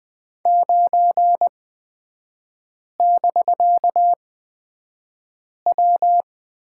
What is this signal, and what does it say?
Morse code, heard as a keyed tone, about 20 wpm, spaced as for 5 wpm, 700 Hz. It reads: 9<BK>W
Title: 250 Lesson 40e - BK - Three Characters 20wpm